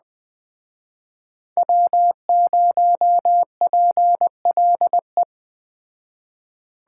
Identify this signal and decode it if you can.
Morse code, heard as a keyed tone, about 21 wpm, 700 Hz. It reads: W0PLE
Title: Speed Racing - USA 1x3 Call Signs 17wpm